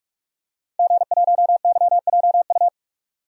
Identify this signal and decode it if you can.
Morse code, heard as a keyed tone, about 41 wpm, 700 Hz. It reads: G1YJU
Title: DXCC-100 Call Signs - 5 Characters - Repeated 3x - 45wpm